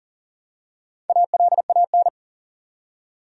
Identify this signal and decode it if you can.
Morse code, heard as a keyed tone, about 40 wpm, 700 Hz. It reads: ALAN
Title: USA Common Names - Max 4 Letters - 40wpm